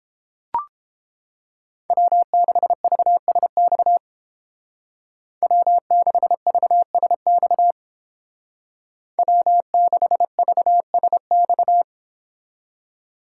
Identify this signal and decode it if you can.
Morse code, heard as a keyed tone, about 29 wpm, 700 Hz. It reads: W6VSX W6VSX W6VSX